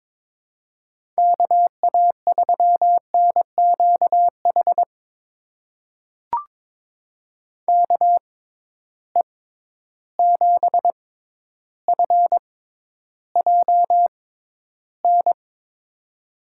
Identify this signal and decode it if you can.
Morse code, heard as a keyed tone, about 22 wpm, spaced as for 11 wpm, 700 Hz. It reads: KA3NQH KE7FJN
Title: USA Call Signs - 6 Characters - 6x Character Spacing - 22wpm